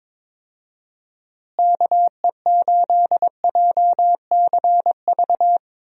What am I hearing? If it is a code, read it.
Morse code, heard as a keyed tone, about 22 wpm, 700 Hz. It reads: KE8JCV
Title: USA Call Signs - 6 Characters - Repeated 2x - 22wpm